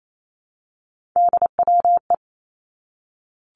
Morse code, heard as a keyed tone, about 28 wpm, 700 Hz. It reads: DWE
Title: Three Challenging Letter-Number Combinations 28wpm